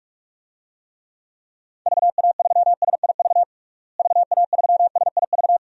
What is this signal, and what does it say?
Morse code, heard as a keyed tone, about 46 wpm, 700 Hz. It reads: VA3SIV VA3SIV
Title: DXCC-100 Call Signs - 6 Characters - Repeated 2x - 45wpm